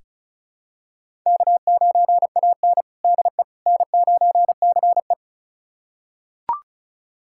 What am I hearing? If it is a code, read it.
Morse code, heard as a keyed tone, about 34 wpm, 700 Hz. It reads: K9AN DE N9CE
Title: USA Callsign DE Callsign - 4 Characters - 35wpm